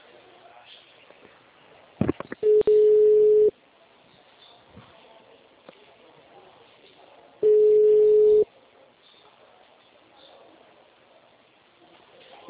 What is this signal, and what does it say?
Ringback tone (Europe, 425 Hz).